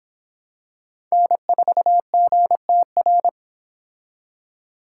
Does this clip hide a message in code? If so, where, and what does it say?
Yes: Morse code from 1.1 to 3.3 s, heard as a keyed tone, about 26 wpm, 700 Hz. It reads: N4GTR